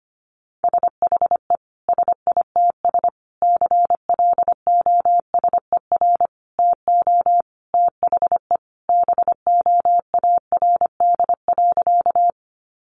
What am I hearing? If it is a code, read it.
Morse code, heard as a keyed tone, about 25 wpm, 700 Hz. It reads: SHE SITS CLOSER TO THE BOARD.
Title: Additional Easy Phrases 25wpm